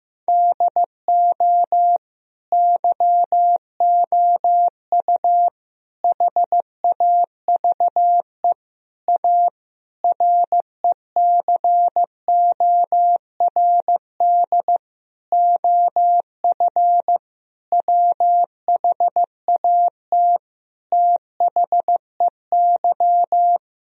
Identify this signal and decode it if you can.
Morse code, heard as a keyed tone, about 15 wpm, 700 Hz. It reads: DO YOU HAVE A RECORD OF WHAT THEY
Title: Sentences from Top 1000 Words - Review Part 3 - 15wpm